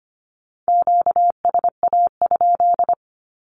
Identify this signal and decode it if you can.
Morse code, heard as a keyed tone, about 25 wpm, 700 Hz. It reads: QSA?
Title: Q-Codes 25wpm